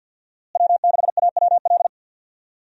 Morse code, heard as a keyed tone, about 51 wpm, 700 Hz. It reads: W6AWL
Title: USA Call Signs - 5 Characters - Repeated 2x - 50wpm